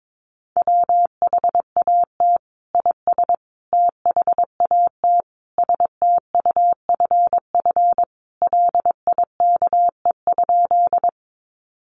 Morse code, heard as a keyed tone, about 22 wpm, 700 Hz. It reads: WHAT IS THAT STUFF LIKE?
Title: Sentences from Top 1400 Words 22wpm